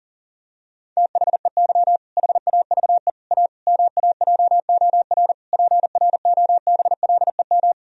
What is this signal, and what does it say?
Morse code, heard as a keyed tone, about 40 wpm, 700 Hz. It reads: THEY HAVE A MAJOR PROBLEM
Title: Sentences from Top 1000 Words - Review Part 4 - 40wpm